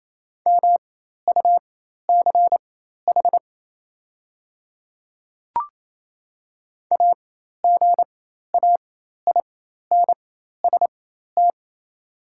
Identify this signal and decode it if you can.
Morse code, heard as a keyed tone, about 28 wpm, spaced as for 13 wpm, 690 Hz. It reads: MUCH AGAINST
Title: 4x Character Spacing - Top 500 Words - 28wpm